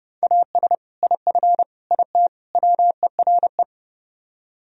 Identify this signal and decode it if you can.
Morse code, heard as a keyed tone, about 38 wpm, 700 Hz. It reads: AS IF IT WERE